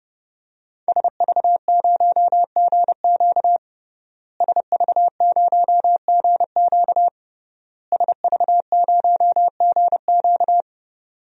Morse code, heard as a keyed tone, about 32 wpm, 700 Hz. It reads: SV0GQ SV0GQ SV0GQ
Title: DXCC-100 Call Signs - 5 Characters - Repeated 3x - 30wpm